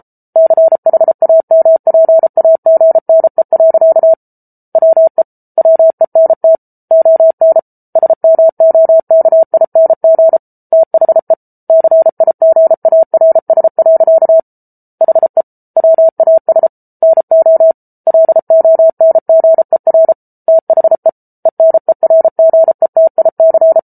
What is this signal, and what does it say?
Morse code, heard as a keyed tone, about 34 wpm, 650 Hz. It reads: CHAMPAGNE. WE WENT ON SMOKING THE CIGARS. HE WAS NO LONGER THE ENERGETIC